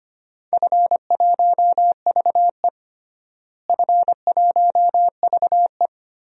Morse code, heard as a keyed tone, about 25 wpm, 700 Hz. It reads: F1VE F1VE
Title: DXCC-100 Call Signs - 4 Characters - Repeated 3x - 25wpm